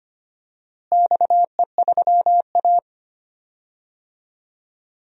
Morse code, heard as a keyed tone, about 25 wpm, 700 Hz. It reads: XE3A
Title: DXCC-100 Call Signs - 4 Characters - Repeated 2x - 25wpm